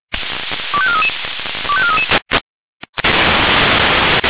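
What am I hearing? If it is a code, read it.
Five-tone selective call (ZVEI), heard as quick runs of five tones, tones of about 80 ms. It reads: 25433 25433
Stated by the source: Five-tone ZVEI selective call received on an SDR.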